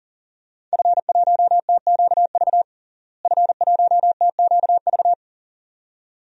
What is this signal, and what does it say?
Morse code, heard as a keyed tone, about 37 wpm, 700 Hz. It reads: F1TQV F1TQV
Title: DXCC-100 Call Signs - 5 Characters - Repeated 2x - 40wpm